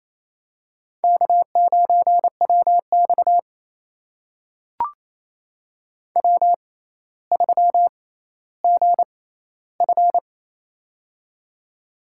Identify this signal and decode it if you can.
Morse code, heard as a keyed tone, about 28 wpm, spaced as for 14 wpm, 700 Hz. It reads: K9WX W3GF
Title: USA Call Signs - 4 Characters - 6x Character Spacing - 28wpm